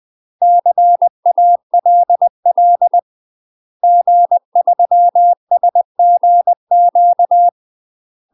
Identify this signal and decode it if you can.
Morse code, heard as a keyed tone, about 20 wpm, 700 Hz. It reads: CALL G3SGQ
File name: UK MASTER QSOs TRAINING / 20WPM / UK MASTER QSOs 0028